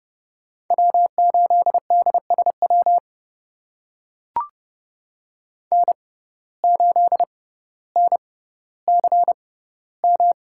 Morse code, heard as a keyed tone, about 32 wpm, spaced as for 16 wpm, 700 Hz. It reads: W8DSW N8NCM
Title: USA Call Signs - 5 Characters - 6x Character Spacing - 30wpm